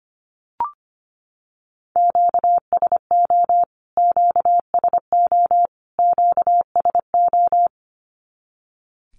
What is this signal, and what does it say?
Morse code, heard as a keyed tone, about 25 wpm, 700 Hz. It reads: QSO QSO QSO
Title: Q-Codes Repeated 3x 25wpm